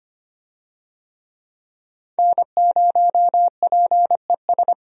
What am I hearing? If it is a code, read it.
Morse code, heard as a keyed tone, about 25 wpm, 700 Hz. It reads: N0PES
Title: USA Call Signs - 5 Characters - Repeated 3x - 25wpm